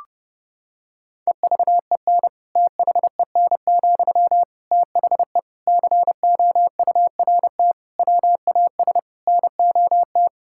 Morse code, heard as a keyed tone, about 30 wpm, 700 Hz. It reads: EVEN THEN, THE COURT WAS NOT
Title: Sentences from Top 1600 Words 30wpm